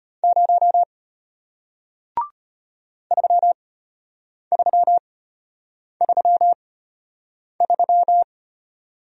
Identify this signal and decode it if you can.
Morse code, heard as a keyed tone, about 34 wpm, spaced as for 8 wpm, 690 Hz. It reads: 0 3333